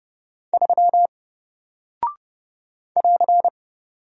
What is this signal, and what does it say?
Morse code, heard as a keyed tone, about 30 wpm, 700 Hz. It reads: 3 <AR>
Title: Single Character (Letter:Number:Punctuation:Prosign) Alt 30wpm